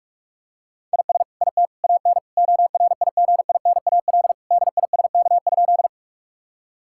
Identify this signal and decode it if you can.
Morse code, heard as a keyed tone, about 46 wpm, 700 Hz. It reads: IS IT AN ORIGINAL DISK?
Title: Sentences from Top 1000 Words - Review Part 6 - 45wpm